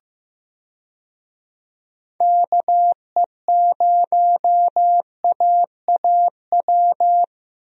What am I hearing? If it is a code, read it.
Morse code, heard as a keyed tone, about 15 wpm, 700 Hz. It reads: KE0AAW
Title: USA Call Signs - 6 Characters - Repeated 2x - 15wpm